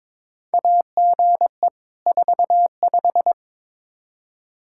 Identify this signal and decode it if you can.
Morse code, heard as a keyed tone, about 22 wpm, 700 Hz. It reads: AGE 45
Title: Practice QSO - Exchange 3 - More Info - 22wpm